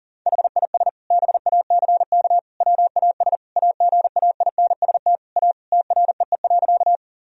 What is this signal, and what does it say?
Morse code, heard as a keyed tone, about 40 wpm, 700 Hz. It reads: HIS BACK WAS AGAINST A TREE.
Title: Sentences from Top 1000 Words - Review Part 1 - 40wpm